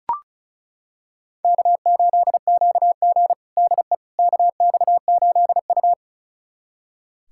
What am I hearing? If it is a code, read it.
Morse code, heard as a keyed tone, about 36 wpm, 690 Hz. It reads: K8QG DE KX8U
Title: USA Callsign DE Callsign - 4 Characters - 35wpm